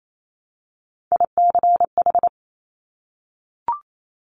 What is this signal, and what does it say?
Morse code, heard as a keyed tone, about 28 wpm, 700 Hz. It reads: ICH
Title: Top Three Letter Combinations in Words 28wpm